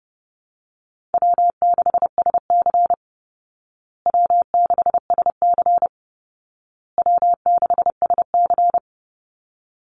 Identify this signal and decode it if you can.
Morse code, heard as a keyed tone, about 30 wpm, 700 Hz. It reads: W6SC W6SC W6SC
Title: Easy Call Signs Repeated 3x - 30wpm